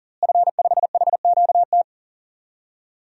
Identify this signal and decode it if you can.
Morse code, heard as a keyed tone, about 40 wpm, 700 Hz. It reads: F5HQT
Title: DXCC-100 Call Signs - 5 Characters - Repeated 2x - 40wpm